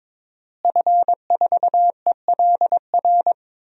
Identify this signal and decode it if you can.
Morse code, heard as a keyed tone, about 22 wpm, 700 Hz. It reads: F4ELR
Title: DXCC-100 Call Signs - 5 Characters - Repeated 3x - 22wpm